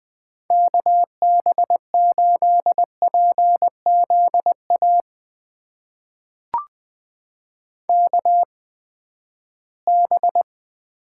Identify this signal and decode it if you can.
Morse code, heard as a keyed tone, about 20 wpm, spaced as for 14 wpm, 700 Hz. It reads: KB8PZA KB
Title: USA Call Signs - 6 Characters - 8x Character Spacing - 20wpm